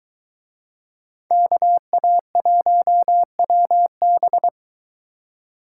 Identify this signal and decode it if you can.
Morse code, heard as a keyed tone, about 23 wpm, 700 Hz. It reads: KA1WB